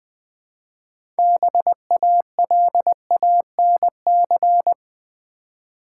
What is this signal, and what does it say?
Morse code, heard as a keyed tone, about 20 wpm, 700 Hz. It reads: BALANC
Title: Build A Word - 2 Letters At A Time - 8-Letter Words - 20wpm